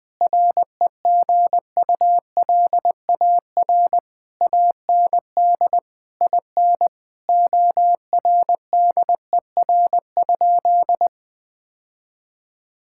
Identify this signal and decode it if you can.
Morse code, heard as a keyed tone, about 21 wpm, 700 Hz. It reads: REGULAR AND IN ORDER?
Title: Sentences from Top 1000 Words - Review Part 6 - 20wpm